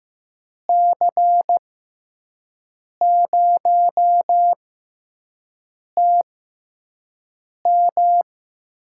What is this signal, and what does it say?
Morse code, heard as a keyed tone, about 15 wpm, spaced as for 5 wpm, 700 Hz. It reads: C0TM
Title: USA Call Signs - 6 Characters - 6x Character Spacing - 15wpm